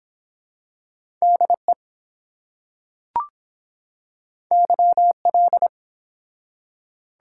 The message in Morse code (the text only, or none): DE YL